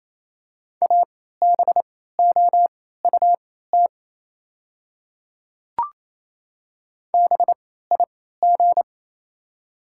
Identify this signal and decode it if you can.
Morse code, heard as a keyed tone, about 28 wpm, spaced as for 16 wpm, 700 Hz. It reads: ABOUT BIG
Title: 3x Character Spacing - Top 200 Words - 28wpm